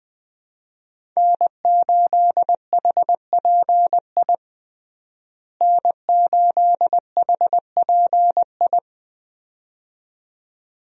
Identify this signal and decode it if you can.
Morse code, heard as a keyed tone, about 21 wpm, 700 Hz. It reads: N8HPI N8HPI
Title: USA Call Signs - 5 Characters - Repeated 2x - 20wpm